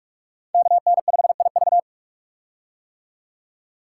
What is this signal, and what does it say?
Morse code, heard as a keyed tone, about 44 wpm, 700 Hz. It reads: KN5IV
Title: Code Groups - 5 Letter:Numbers - 45wpm